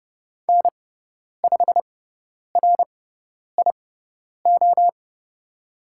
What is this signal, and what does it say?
Morse code, heard as a keyed tone, about 29 wpm, spaced as for 10 wpm, 700 Hz. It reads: N5RIO
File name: Lesson 18g - Letter m - Call Signs 30wpm